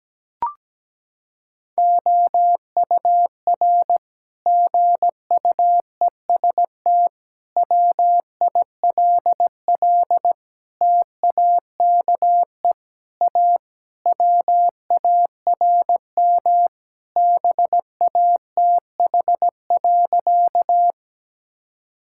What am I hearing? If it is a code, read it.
Morse code, heard as a keyed tone, about 17 wpm, 700 Hz. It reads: OUR GUEST WILL TAKE A WARM BATH.
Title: Sentences from Top 1600 Words 17wpm